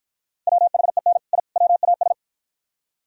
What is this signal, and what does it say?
Morse code, heard as a keyed tone, about 53 wpm, 700 Hz. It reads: WHEN I WAS